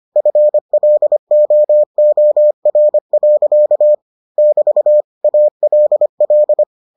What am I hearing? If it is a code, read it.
Morse code, heard as a keyed tone, about 25 wpm, 600 Hz. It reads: FLOOR. <BT> ALL